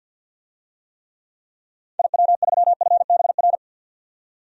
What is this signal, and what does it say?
Morse code, heard as a keyed tone, about 47 wpm, 700 Hz. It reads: IW3FBR
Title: DXCC-100 Call Signs - 6 Characters - 50wpm